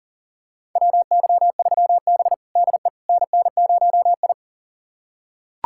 Morse code, heard as a keyed tone, about 40 wpm, 700 Hz. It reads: WY3B DE NN0I
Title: USA Callsign DE Callsign - 4 Characters - 40wpm